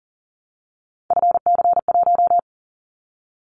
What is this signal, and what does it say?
Morse code, heard as a keyed tone, about 41 wpm, 700 Hz. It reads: FC1
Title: Callsign Prefixes for Top 25 DXCC - 40wpm